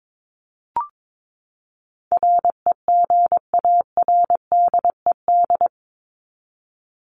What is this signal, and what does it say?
Morse code, heard as a keyed tone, about 22 wpm, 700 Hz. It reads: REGARDED
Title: ED Suffix - Top 100 Words with Base Words 22wpm